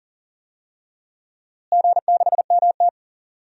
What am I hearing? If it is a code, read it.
Morse code, heard as a keyed tone, about 38 wpm, 700 Hz. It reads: G6MT
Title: DXCC-100 Call Signs - 4 Characters - Repeated 2x - 40wpm